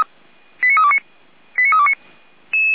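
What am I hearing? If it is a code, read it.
Five-tone selective call (ZVEI), heard as quick runs of five tones, tones of about 70 ms. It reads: 89329 89329
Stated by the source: ZVEI five-tone selective call 89330 received via SDR.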